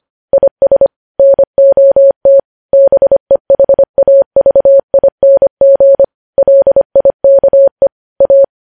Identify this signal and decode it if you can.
Morse code, heard as a keyed tone, about 25 wpm, 550 Hz. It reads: IS NOT BEHAVING LIKE A
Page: news headlines in morse code at 25 wpm